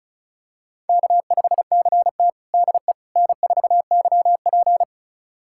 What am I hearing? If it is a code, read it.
Morse code, heard as a keyed tone, about 36 wpm, 700 Hz. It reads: K5CT DE N4YP
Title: USA Callsign DE Callsign - 4 Characters - 35wpm